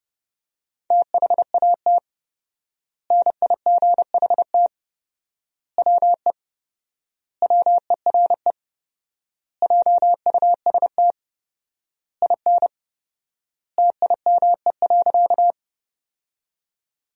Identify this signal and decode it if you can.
Morse code, heard as a keyed tone, about 30 wpm, 700 Hz. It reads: THAT NIGHT WE WERE JUST IN TIME.
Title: 4x Word-Spacing - Sentences from Top 300 Words - 30wpm